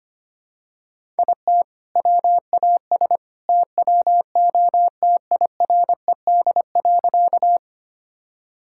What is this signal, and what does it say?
Morse code, heard as a keyed tone, about 25 wpm, 700 Hz. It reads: IT WAS TWO TIRED.
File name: practice / 25wpm / All-Age Appropriate Jokes - 25wpm